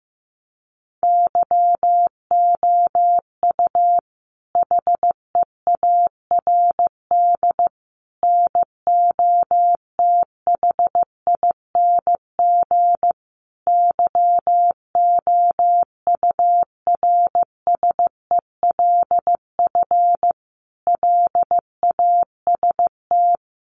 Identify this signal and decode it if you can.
Morse code, heard as a keyed tone, about 15 wpm, 700 Hz. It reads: YOU HEARD NOTHING YOURSELF LAST
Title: Sentences from Top 600 Words 15wpm